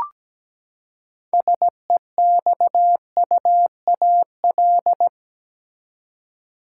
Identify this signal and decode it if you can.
Morse code, heard as a keyed tone, about 17 wpm, 700 Hz. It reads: SEXUAL
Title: Six Letter Words - 17wpm